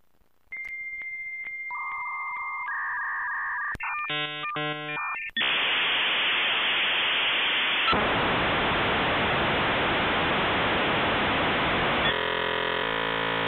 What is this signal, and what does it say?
Dial-up modem handshake.
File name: The-Sound-of-dial-up-Internet